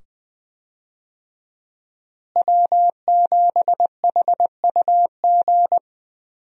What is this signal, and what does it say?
Morse code, heard as a keyed tone, about 21 wpm, 700 Hz. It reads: W7HUG
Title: USA Call Signs - 5 Characters - Repeated 2x - 20wpm